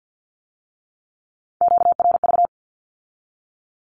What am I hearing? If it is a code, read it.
Morse code, heard as a keyed tone, about 50 wpm, 700 Hz. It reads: QRV
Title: Q-Codes with Spoken Meaning - 50wpm